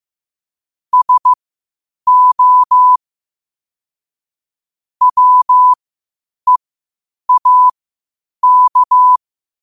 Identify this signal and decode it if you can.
Morse code, heard as a keyed tone, about 15 wpm, spaced as for 8 wpm, 1000 Hz. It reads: SO WEAK
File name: qotdfiles / 8wpm